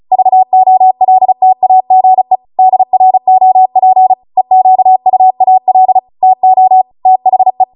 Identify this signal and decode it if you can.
Morse code, heard as a keyed tone, about 36 wpm, 750 Hz. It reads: VOLTAGE DROP EQUAL TO THE